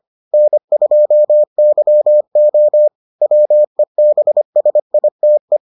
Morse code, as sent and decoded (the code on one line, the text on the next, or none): -. ..--- -.-- --- / .-- . -... ... .. - .
N2YO WEBSITE